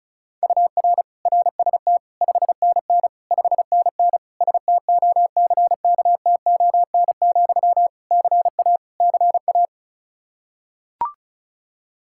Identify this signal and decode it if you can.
Morse code, heard as a keyed tone, about 34 wpm, 700 Hz. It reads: UR RST 5NN 5NN STOCKTON, CA CA
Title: Practice QSO - Exchange 1 - Beginning - 35wpm